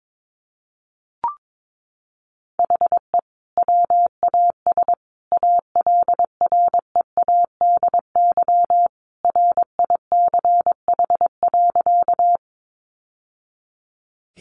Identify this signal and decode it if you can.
Morse code, heard as a keyed tone, about 22 wpm, 700 Hz. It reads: HE WAS ALREADY RICH.
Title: Sentences from Top 1100 Words 22wpm